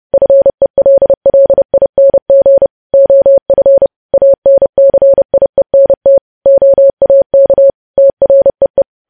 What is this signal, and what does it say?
Morse code, heard as a keyed tone, about 30 wpm, 550 Hz. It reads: FELLING OF ANCIENT OAK TREE
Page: news headlines in morse code at 30 wpm